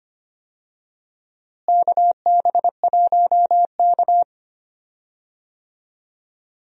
Morse code, heard as a keyed tone, about 25 wpm, 700 Hz. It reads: KB1K
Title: USA Call Signs - 4 Characters - Repeated 2x - 25wpm